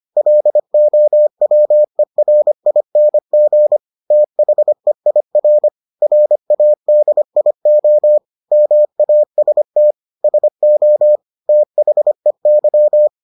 Morse code, heard as a keyed tone, about 25 wpm, 600 Hz. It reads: LOWERING THEIR RADIO MAST SO THEY